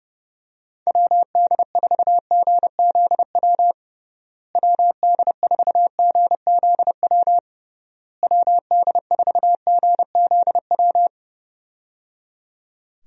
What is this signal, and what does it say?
Morse code, heard as a keyed tone, about 30 wpm, 700 Hz. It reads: WD4GZW WD4GZW WD4GZW
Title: USA Call Signs - 6 Characters - Repeated 3x - 30wpm